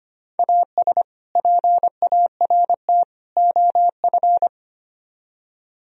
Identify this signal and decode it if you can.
Morse code, heard as a keyed tone, about 31 wpm, 700 Hz. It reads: AS PART OF
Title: Speed-Racing - Sets of 3 Words 25wpm